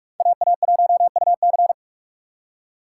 Morse code, heard as a keyed tone, about 46 wpm, 700 Hz. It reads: AA1UC
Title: K1USN SST QSO Practice - 45wpm